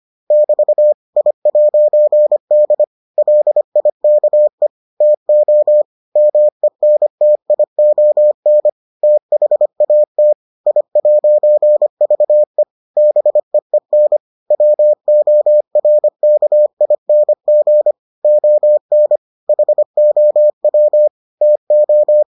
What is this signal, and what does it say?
Morse code, heard as a keyed tone, about 25 wpm, 600 Hz. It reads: <BT> I'D LIKE TO MENTION THAT I'VE BEEN WORKING ON HOW TO